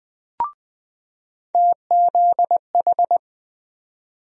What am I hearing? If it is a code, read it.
Morse code, heard as a keyed tone, about 20 wpm, 700 Hz. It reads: TZH
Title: USA Call Sign Postfixes - 3 Letters 20wpm